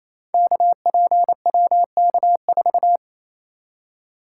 Morse code, heard as a keyed tone, about 28 wpm, 700 Hz. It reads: KPWK4
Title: Code Groups - 5 Letter:Numbers - 28wpm